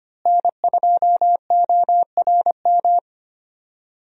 Morse code, heard as a keyed tone, about 25 wpm, 700 Hz. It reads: N2ORM
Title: USA Call Signs - 5 Characters - Repeated 3x - 25wpm